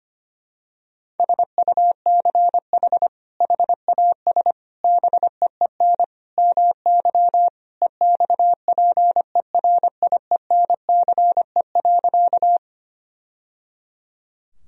Morse code, heard as a keyed tone, about 25 wpm, 700 Hz. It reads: SUCH HAS BEEN MY EXPERIENCE.
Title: Sentences from Top 1000 Words - Review Part 3 - 25wpm